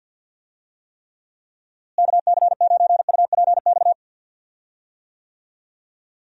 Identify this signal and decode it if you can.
Morse code, heard as a keyed tone, about 51 wpm, 700 Hz. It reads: KC9UPX
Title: USA Call Signs - 6 Characters - Repeated 2x - 50wpm